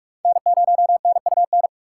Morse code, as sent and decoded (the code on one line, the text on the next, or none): -. ----- -. ..- -.
N0NUN